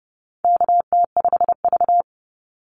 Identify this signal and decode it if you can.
Morse code, heard as a keyed tone, about 30 wpm, 700 Hz. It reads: KT5V
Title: Call Signs Repeated 2x - 30wpm